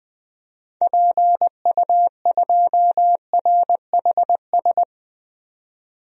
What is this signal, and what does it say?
Morse code, heard as a keyed tone, about 21 wpm, 700 Hz. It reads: PU2RHS
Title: DXCC-100 Call Signs - 6 Characters - 20wpm